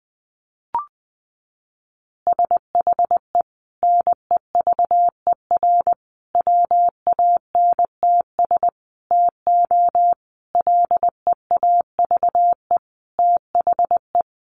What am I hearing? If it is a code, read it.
Morse code, heard as a keyed tone, about 20 wpm, 700 Hz. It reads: SHE NEVER WANTS TO LEAVE THE
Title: Easy Phrases 20wpm